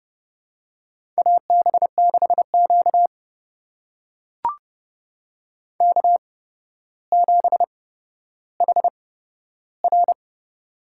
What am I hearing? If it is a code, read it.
Morse code, heard as a keyed tone, about 30 wpm, spaced as for 13 wpm, 700 Hz. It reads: AB6Q K7HR
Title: USA Call Signs - 8x Character Spacing - 30wpm